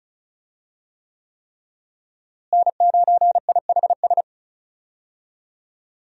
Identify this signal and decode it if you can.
Morse code, heard as a keyed tone, about 36 wpm, 700 Hz. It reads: N9IHS
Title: USA Call Signs - 5 Characters - Repeated 3x - 35wpm